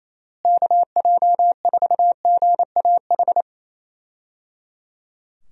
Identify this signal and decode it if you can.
Morse code, heard as a keyed tone, about 28 wpm, 700 Hz. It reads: KJ4GAH
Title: USA Call Signs - 6 Characters - Repeated 3x - 28wpm